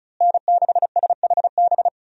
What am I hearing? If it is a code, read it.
Morse code, heard as a keyed tone, about 36 wpm, 700 Hz. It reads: N6SHB
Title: USA Call Signs - 5 Characters - Repeated 2x - 35wpm